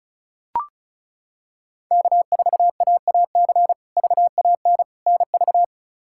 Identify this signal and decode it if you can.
Morse code, heard as a keyed tone, about 34 wpm, 700 Hz. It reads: K4AAC VAN NV
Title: K1USN SST QSO Practice - 35wpm